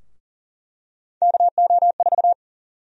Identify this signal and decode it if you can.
Morse code, heard as a keyed tone, about 40 wpm, 700 Hz. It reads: KO4
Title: USA Call Sign Prefixes - Repeated 2x - 40wpm